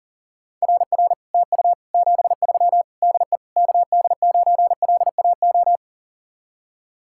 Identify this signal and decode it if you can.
Morse code, heard as a keyed tone, about 50 wpm, 700 Hz. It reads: RR TU 73 DE KD9LAO
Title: Speed-Racing Common QSO Phrases Encore - 40wpm